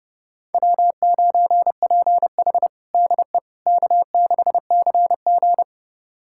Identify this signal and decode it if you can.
Morse code, heard as a keyed tone, about 30 wpm, 700 Hz. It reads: W9PH DE K6CG
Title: USA Callsign DE Callsign - 4 Characters - 30wpm